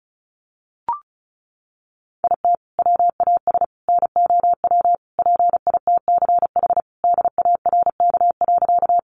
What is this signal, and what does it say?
Morse code, heard as a keyed tone, about 34 wpm, 700 Hz. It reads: IT WAS NOW PITCH DARK.
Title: Sentences from Top 1300 Words 35wpm